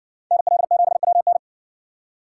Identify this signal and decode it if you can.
Morse code, heard as a keyed tone, about 58 wpm, 700 Hz. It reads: NL7WN